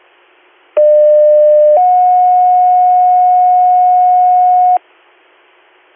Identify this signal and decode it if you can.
Two-tone page, heard as a one-second tone, then a three-second tone, about 600 then 730 Hz, with noise. It is heard from 0.8 to 4.8 s.